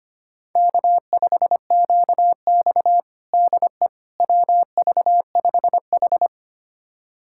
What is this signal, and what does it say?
Morse code, heard as a keyed tone, about 25 wpm, 700 Hz. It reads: K5QX DE WV5H
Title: USA Callsign DE Callsign - 4 Characters - 25wpm